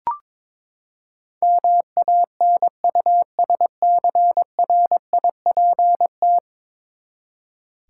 Morse code, heard as a keyed tone, about 22 wpm, 700 Hz. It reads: MANUSCRIPT
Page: morse-code-ninja-practice-22wpm